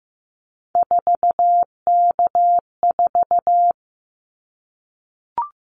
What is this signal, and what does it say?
Morse code, heard as a keyed tone, about 15 wpm, 700 Hz. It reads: 4K4
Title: Callsign Prefixes for Top 100 DXCC - 15wpm